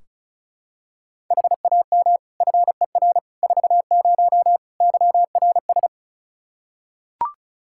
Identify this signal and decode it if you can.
Morse code, heard as a keyed tone, about 34 wpm, 700 Hz. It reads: HAM FER 40 YRS
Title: Practice QSO - Exchange 3 - More Info - 35wpm